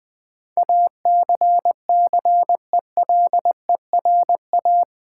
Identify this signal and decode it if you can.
Morse code, heard as a keyed tone, about 20 wpm, 700 Hz. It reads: ACCELERA
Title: Build A Word - 4 Letters At A Time - 10-Letter Words - 20wpm